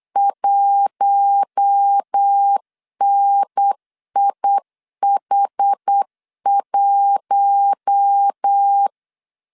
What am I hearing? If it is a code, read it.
Morse code, heard as a keyed tone, about 8 wpm, 800 Hz. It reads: 1NIH1
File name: LEZIONI / 6 PKX0 / 6 PKX 8wpm 3041